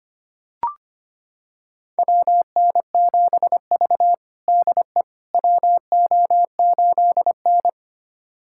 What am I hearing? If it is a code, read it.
Morse code, heard as a keyed tone, about 25 wpm, 700 Hz. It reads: WN7V DE WO8N
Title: USA Callsign DE Callsign - 4 Characters - 25wpm